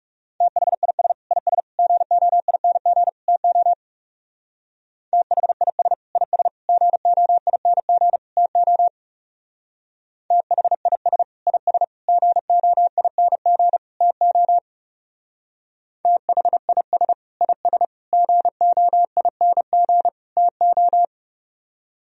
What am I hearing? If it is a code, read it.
Morse code, heard as a keyed tone, about 38 wpm, 700 Hz. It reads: THIS IS GOING TO THIS IS GOING TO THIS IS GOING TO THIS IS GOING TO
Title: Speed-Racing - Sets of 4 Words 30wpm